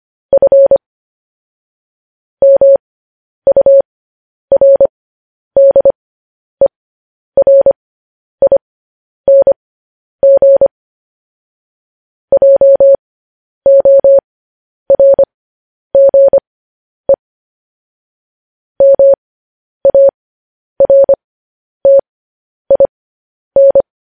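Morse code, heard as a keyed tone, about 25 wpm, spaced as for 10 wpm, 550 Hz. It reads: F MURDERING JORGE MARTIN
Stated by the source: Morse code news headlines broadcast.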